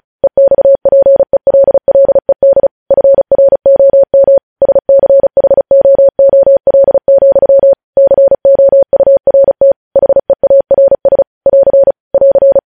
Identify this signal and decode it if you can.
Morse code, heard as a keyed tone, about 36 wpm, 550 Hz. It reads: EXPELLED FROM SCHOOL, COURT HEARS <AR> <AR>
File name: podcast / News headlines-1x-35x35